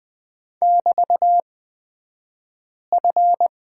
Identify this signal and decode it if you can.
Morse code, heard as a keyed tone, about 20 wpm, spaced as for 5 wpm, 690 Hz. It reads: <BT>F
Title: 228 Lesson 37d - Letter x - Two Characters 20wpm